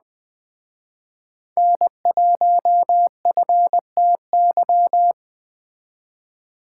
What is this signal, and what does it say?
Morse code, heard as a keyed tone, about 20 wpm, 700 Hz. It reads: N1FTY